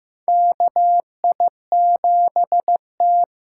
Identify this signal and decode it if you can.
Morse code, heard as a keyed tone, about 15 wpm, 700 Hz. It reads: KI7T
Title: USA Call Signs - 6 Characters - 6x Character Spacing - 15wpm